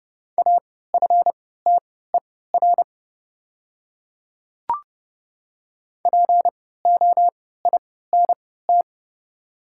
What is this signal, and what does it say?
Morse code, heard as a keyed tone, about 30 wpm, spaced as for 17 wpm, 690 Hz. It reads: AFTER POINT
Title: 3x Character Spacing - Top 300 Words - 30wpm